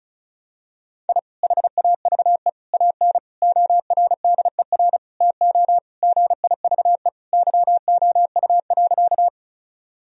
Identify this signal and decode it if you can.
Morse code, heard as a keyed tone, about 36 wpm, 700 Hz. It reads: I HAVE AN ORDER TO GIVE YOU.
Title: Sentences from Top 1000 Words - Review Part 2 - 35wpm